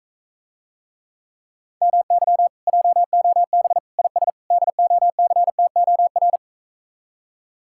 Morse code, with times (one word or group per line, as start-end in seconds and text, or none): 1.8-2.5 MY
2.7-3.8 JOB
4.0-4.3 IS
4.5-6.4 DOCTOR